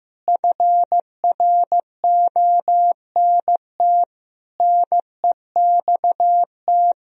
Morse code, heard as a keyed tone, about 15 wpm, 700 Hz. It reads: FRONT NEXT
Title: Sentences from Top 1000 Words - Review Part 2 - 15wpm